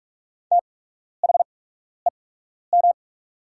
Morse code, heard as a keyed tone, about 44 wpm, spaced as for 12 wpm, 690 Hz. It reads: THEM
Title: 8x Character Spacing - Top 200 Words - 45wpm